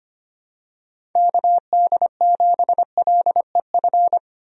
Morse code, heard as a keyed tone, about 25 wpm, 700 Hz. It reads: KD7LEF
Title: USA Call Signs - 6 Characters - Repeated 2x - 25wpm